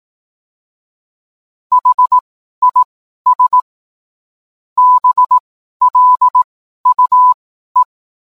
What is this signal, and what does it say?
Morse code, heard as a keyed tone, about 18 wpm, spaced as for 12 wpm, 1000 Hz. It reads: HIS BLUE